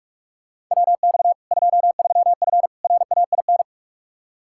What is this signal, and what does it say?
Morse code, heard as a keyed tone, about 46 wpm, 700 Hz. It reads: WX 23F RAIN
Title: Practice QSO - Exchange 2 - Info - 45wpm